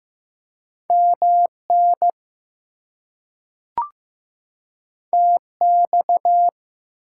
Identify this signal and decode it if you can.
Morse code, heard as a keyed tone, about 15 wpm, 700 Hz. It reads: MN TX
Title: US States and Canadian Provinces 15wpm